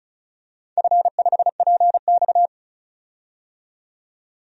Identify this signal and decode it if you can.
Morse code, heard as a keyed tone, about 36 wpm, 700 Hz. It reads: F5PX
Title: DXCC-100 Call Signs - 4 Characters - 35wpm